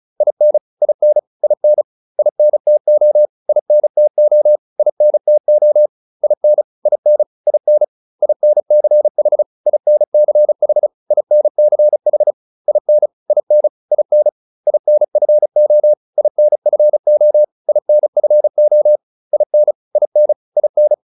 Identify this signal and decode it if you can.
Morse code, heard as a keyed tone, about 34 wpm, 600 Hz. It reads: IN IN IN INTO INTO INTO IN IN IN INCH INCH INCH IN IN IN INFO INFO INFO IN IN IN